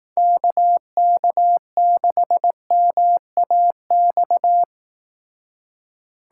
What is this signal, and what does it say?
Morse code, heard as a keyed tone, about 18 wpm, 700 Hz. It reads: KK6MAX
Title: Speed Racing - USA Call Signs 15wpm